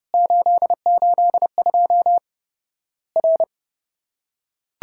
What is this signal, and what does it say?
Morse code, heard as a keyed tone, about 31 wpm, 700 Hz. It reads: 882 R
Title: Worked All Europe DX Contest - QTC Exchange - 30wpm